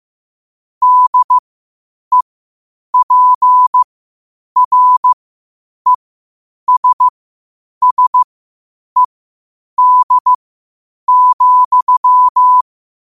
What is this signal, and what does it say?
Morse code, heard as a keyed tone, about 15 wpm, spaced as for 8 wpm, 1000 Hz. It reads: DEPRESSED,
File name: qotdfiles / 8wpm